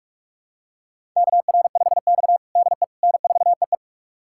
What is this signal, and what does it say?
Morse code, heard as a keyed tone, about 42 wpm, 700 Hz. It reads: KR5X DE N4EE
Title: USA Callsign DE Callsign - 4 Characters - 45wpm